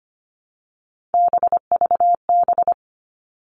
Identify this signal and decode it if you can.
Morse code, heard as a keyed tone, about 25 wpm, 700 Hz. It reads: BVB
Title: All Cut Numbers - 3-Digit 25wpm